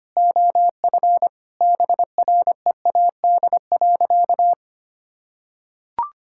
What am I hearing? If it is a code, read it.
Morse code, heard as a keyed tone, about 25 wpm, 700 Hz. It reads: OF BREAD.
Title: Sentences from Top 1600 Words 25wpm